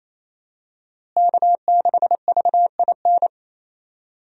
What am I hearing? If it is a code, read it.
Morse code, heard as a keyed tone, about 28 wpm, 700 Hz. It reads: K6VIN
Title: USA Call Signs - 5 Characters - Repeated 3x - 28wpm